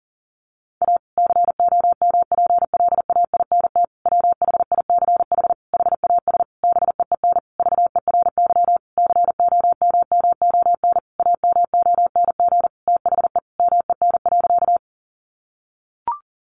Morse code, heard as a keyed tone, about 37 wpm, 700 Hz. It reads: A COMPLAINT WHICH HAS BEEN VERY COMMON AMONG THE MEN.
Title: Sentences from Top 1400 Words 40wpm